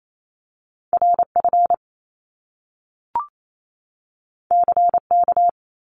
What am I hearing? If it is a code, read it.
Morse code, heard as a keyed tone, about 28 wpm, 700 Hz. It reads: R F C K
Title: Top Two Letter Combinations in Words 28wpm